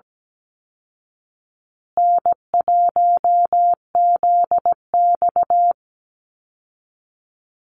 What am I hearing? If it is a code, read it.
Morse code, heard as a keyed tone, about 17 wpm, 700 Hz. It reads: N1ZX
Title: Easy Call Signs Repeated 2x - 17wpm